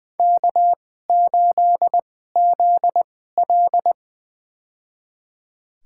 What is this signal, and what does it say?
Morse code, heard as a keyed tone, about 20 wpm, spaced as for 14 wpm, 700 Hz. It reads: K8ZL
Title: USA Call Signs - 4 Characters - 2x Character Spacing - 20wpm